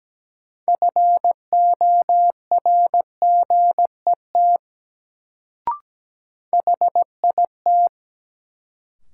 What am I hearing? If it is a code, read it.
Morse code, heard as a keyed tone, about 17 wpm, 700 Hz. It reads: FORGET HIT
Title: Sentences from Top 1200 Words - New Word Drill - 17wpm